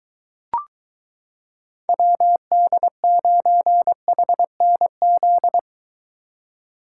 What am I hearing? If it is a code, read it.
Morse code, heard as a keyed tone, about 23 wpm, 700 Hz. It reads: WD9HNZ